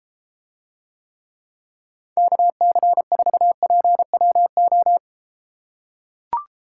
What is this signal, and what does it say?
Morse code, heard as a keyed tone, about 33 wpm, 700 Hz. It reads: KC4PWO